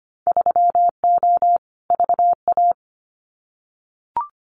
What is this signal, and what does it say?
Morse code, heard as a keyed tone, about 25 wpm, 700 Hz. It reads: 3O VA
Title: Winter Field Day Exchanges 25wpm